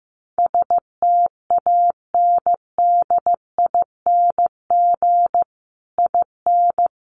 Morse code, heard as a keyed tone, about 15 wpm, 700 Hz. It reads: STANDING IN
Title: Sets of 5 Words Encore2 15wpm